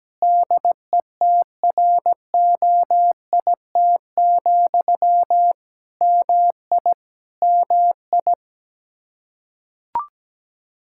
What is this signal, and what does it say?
Morse code, heard as a keyed tone, about 17 wpm, 700 Hz. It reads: DETROIT, MI MI
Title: Practice QSO - Exchange 1 - Beginning - 17wpm